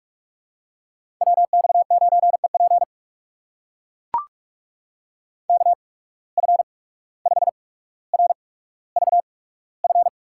Morse code, heard as a keyed tone, about 44 wpm, spaced as for 18 wpm, 700 Hz. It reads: WX9EP KF5RVF
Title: USA Call Signs - 8x Character Spacing - 45wpm